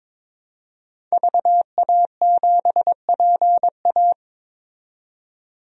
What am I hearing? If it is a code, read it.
Morse code, heard as a keyed tone, about 22 wpm, 700 Hz. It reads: VA7PA
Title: DXCC-100 Call Signs - 5 Characters - 22wpm